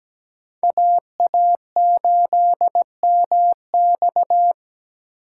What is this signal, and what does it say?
Morse code, heard as a keyed tone, about 20 wpm, 700 Hz. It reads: AA8MX
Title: Speed Racing - USA 2x2 Call Signs 17wpm